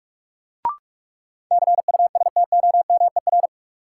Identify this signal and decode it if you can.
Morse code, heard as a keyed tone, about 46 wpm, 700 Hz. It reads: CUSTOMER
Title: Sentences from Top 1100 Words - New Word Drill Alt - 45wpm